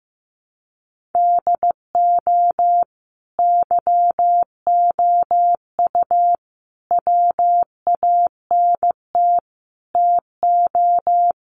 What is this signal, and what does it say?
Morse code, heard as a keyed tone, about 15 wpm, 700 Hz. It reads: DO YOU WANT TO
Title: Sentences from Top 1400 Words 15wpm